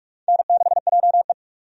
Morse code, heard as a keyed tone, about 43 wpm, 700 Hz. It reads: N6JE
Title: USA Call Signs - 4 Characters - 45wpm